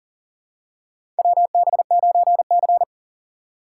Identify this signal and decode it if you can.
Morse code, heard as a keyed tone, about 40 wpm, 700 Hz. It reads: WB9C
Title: USA Call Signs - 4 Characters - 2x Character Spacing - 40wpm